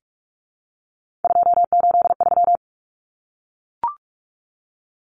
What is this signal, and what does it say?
Morse code, heard as a keyed tone, about 44 wpm, 700 Hz. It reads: 283
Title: Numbers - 3-Digit 45wpm